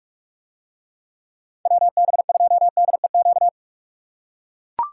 Morse code, heard as a keyed tone, about 44 wpm, 700 Hz. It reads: WB2DEQ